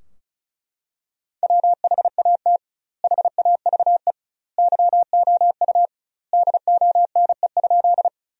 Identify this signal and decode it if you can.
Morse code, heard as a keyed tone, about 36 wpm, 700 Hz. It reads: WHAT HAVE YOU DONE?
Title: 2x Word-Spacing - Sentences from Top 500 Words - 35wpm